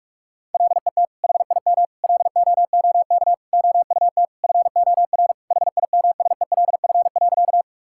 Morse code, heard as a keyed tone, about 46 wpm, 700 Hz. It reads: LET HIM LOOK OUT FOR HIMSELF.
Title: Sentences from Top 1600 Words 45wpm